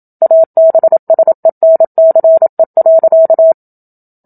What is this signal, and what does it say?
Morse code, heard as a keyed tone, about 27 wpm, 650 Hz. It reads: ABSENCE.
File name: Book Two / 01-Chapter 1 - 27 WPM